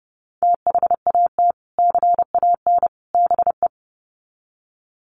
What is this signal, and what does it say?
Morse code, heard as a keyed tone, about 30 wpm, 700 Hz. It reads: THAT CAN BE
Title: Sets of 3 Words 30wpm